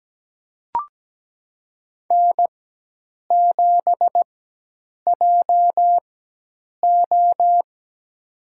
Morse code, heard as a keyed tone, about 17 wpm, spaced as for 8 wpm, 700 Hz. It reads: N7JO
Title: USA Call Signs - 4 Characters - 4x Character Spacing - 17wpm